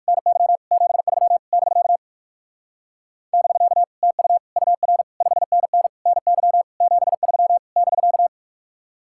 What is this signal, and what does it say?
Morse code, heard as a keyed tone, about 59 wpm, 700 Hz. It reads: NY 73 <BK> <BK> TU UR 5NN NY 73 <BK>